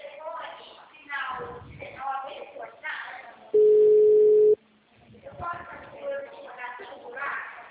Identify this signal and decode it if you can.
Ringback tone (Europe, 425 Hz), with speech.
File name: out- -2023- - - .93410